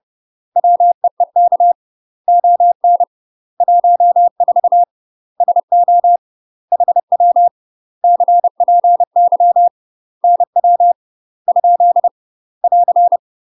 Morse code, heard as a keyed tone, about 30 wpm, 700 Hz. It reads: WEEK ON 14 SO HW CPY NW ? <AR>
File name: UK MASTER QSOs TRAINING / 30WPM / UK MASTER QSOs 0047